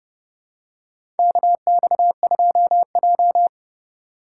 Morse code, heard as a keyed tone, about 32 wpm, 700 Hz. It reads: KX2J
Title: USA Call Signs - 2x Character Spacing - 30wpm